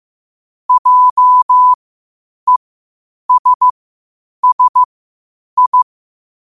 Morse code, heard as a keyed tone, about 15 wpm, spaced as for 8 wpm, 1000 Hz. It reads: JESSI
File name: qotdfiles / 8wpm